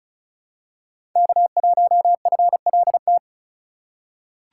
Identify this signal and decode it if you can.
Morse code, heard as a keyed tone, about 34 wpm, 700 Hz. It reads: K1FLT
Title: USA Call Signs - 5 Characters - 4x Character Spacing - 35wpm